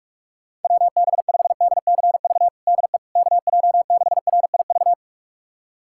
Morse code, heard as a keyed tone, about 41 wpm, 700 Hz. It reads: WB5DCV DE KJ6RIV
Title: USA Callsign DE Callsign - 6 Characters - 45wpm